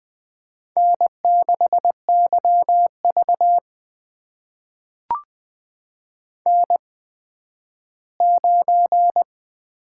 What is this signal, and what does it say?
Morse code, heard as a keyed tone, about 20 wpm, spaced as for 12 wpm, 700 Hz. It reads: N6YV N9
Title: USA Call Signs - 8x Character Spacing - 20wpm